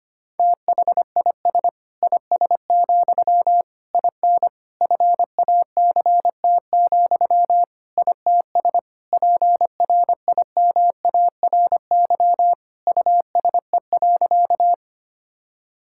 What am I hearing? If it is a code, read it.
Morse code, heard as a keyed tone, about 25 wpm, 700 Hz. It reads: THIS IS, IN FACT, ITS PRIMARY USE.
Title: Sentences from Top 1000 Words - Review Part 5 - 25wpm